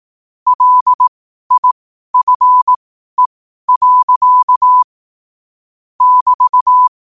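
Morse code, heard as a keyed tone, about 18 wpm, spaced as for 12 wpm, 1000 Hz. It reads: LIFE. <BT>
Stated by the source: Morse code at 12 wpm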